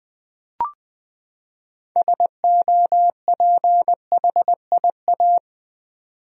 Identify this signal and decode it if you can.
Morse code, heard as a keyed tone, about 20 wpm, 700 Hz. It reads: SOPHIA
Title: USA Common Names - 20wpm